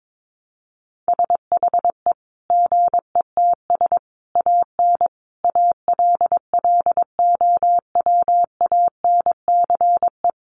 Morse code, heard as a keyed tone, about 22 wpm, 700 Hz. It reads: SHE GETS AN ALLOWANCE
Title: Easy Phrases 22wpm